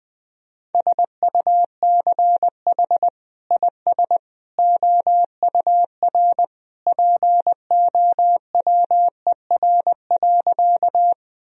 Morse code, heard as a keyed tone, about 20 wpm, 700 Hz. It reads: SUCH IS OUR POWER.
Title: Sentences from Top 1000 Words - Review Part 2 - 20wpm